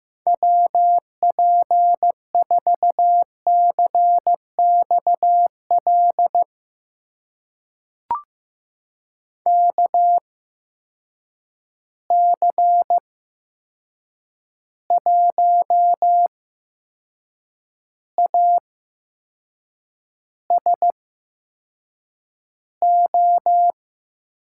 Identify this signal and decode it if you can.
Morse code, heard as a keyed tone, about 15 wpm, spaced as for 6 wpm, 700 Hz. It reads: WP4CXL KC1ASO
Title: USA Call Signs - 6 Characters - 8x Character Spacing - 15wpm